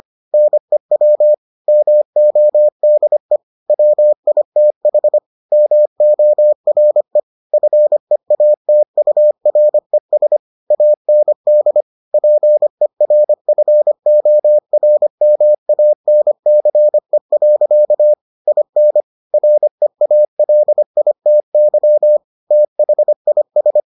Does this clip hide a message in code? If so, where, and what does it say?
Yes: Morse code from 0.3 to 23.8 s, heard as a keyed tone, about 25 wpm, 600 Hz. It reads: NEW MODE WITH MORE FEATURES AND PERFORMANCE. IN REALITY THIS